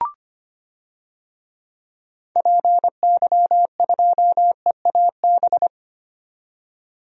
Morse code, heard as a keyed tone, about 25 wpm, 700 Hz. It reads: PY2EAB
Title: DXCC-100 Call Signs - 6 Characters - 25wpm